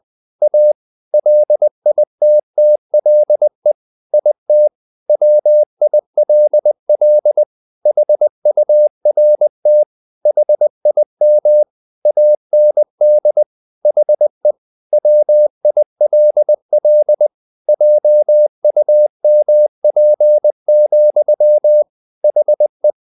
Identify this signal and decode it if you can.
Morse code, heard as a keyed tone, about 20 wpm, 600 Hz. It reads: A LITTLE IT WILL HURT HIM AND HE WILL JUMP, HE
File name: the old man and the sea / the old man and the sea part 1 20wpm